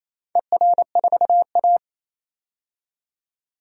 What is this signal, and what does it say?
Morse code, heard as a keyed tone, about 28 wpm, 700 Hz. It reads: ER4A
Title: DXCC-100 Call Signs - 4 Characters - 28wpm